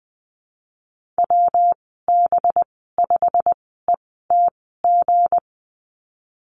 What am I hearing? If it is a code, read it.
Morse code, heard as a keyed tone, about 20 wpm, spaced as for 14 wpm, 700 Hz. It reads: WB5ETG
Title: 193 Lesson 31g - Letter v - Call Signs 20wpm